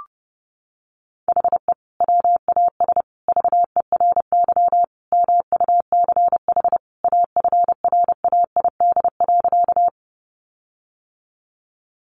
Morse code, heard as a keyed tone, about 30 wpm, 700 Hz. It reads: HE WAS VERY MUCH AFRAID.
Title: Sentences from Top 1500 Words 30wpm